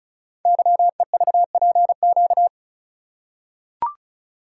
Morse code, heard as a keyed tone, about 36 wpm, 700 Hz. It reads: YEVPQ
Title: Code Groups - 5 Letters - 35wpm